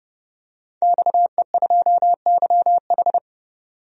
Morse code, heard as a keyed tone, about 30 wpm, 700 Hz. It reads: XE2YH
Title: DXCC-100 Call Signs - 5 Characters - 30wpm